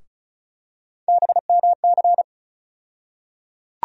Morse code, heard as a keyed tone, about 36 wpm, 700 Hz. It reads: BMC
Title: USA Call Sign Postfixes - 3 Letters 35wpm